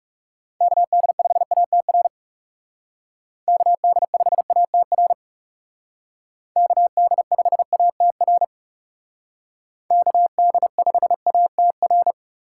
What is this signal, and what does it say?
Morse code, heard as a keyed tone, about 37 wpm, 700 Hz. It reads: KD5ATR KD5ATR KD5ATR KD5ATR
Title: Speed Racing - USA 2x3 Call Signs 30wpm